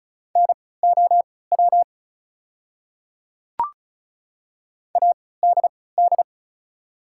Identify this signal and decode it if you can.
Morse code, heard as a keyed tone, about 35 wpm, spaced as for 20 wpm, 700 Hz. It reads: NOW ADD
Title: 3x Character Spacing - Top 200 Words - 35wpm